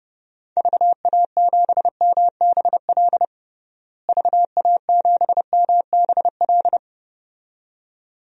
Morse code, heard as a keyed tone, about 32 wpm, 700 Hz. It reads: VA7MBL VA7MBL
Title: DXCC-100 Call Signs - 6 Characters - Repeated 2x - 30wpm